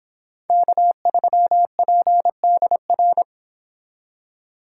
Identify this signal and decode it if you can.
Morse code, heard as a keyed tone, about 29 wpm, 700 Hz. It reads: K3PDR